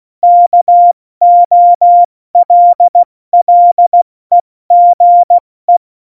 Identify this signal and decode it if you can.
Morse code, heard as a keyed tone, about 16 wpm, spaced as for 14 wpm, 700 Hz. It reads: KOLLEGE
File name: Morsekurs / Witze / Witz 14. August 2018, 70 F80